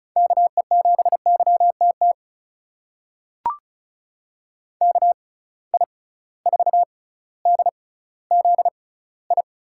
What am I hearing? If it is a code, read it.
Morse code, heard as a keyed tone, about 36 wpm, spaced as for 18 wpm, 700 Hz. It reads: KE7YTT KI4DZI
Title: USA Call Signs - 6 Characters - 6x Character Spacing - 35wpm